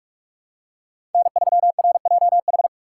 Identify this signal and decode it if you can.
Morse code, heard as a keyed tone, about 47 wpm, 700 Hz. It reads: N3RJH
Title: USA Call Signs - 5 Characters - 6x Character Spacing - 45wpm